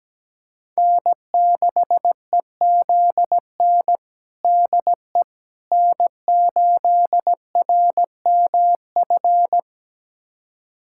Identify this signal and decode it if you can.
Morse code, heard as a keyed tone, about 17 wpm, 700 Hz. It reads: N6EZN DE N8RMF
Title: USA Callsign DE Callsign - 5 Characters - 17wpm